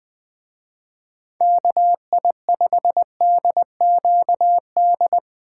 Morse code, heard as a keyed tone, about 23 wpm, 700 Hz. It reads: KI5DQD
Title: Speed Racing - USA Call Signs 20wpm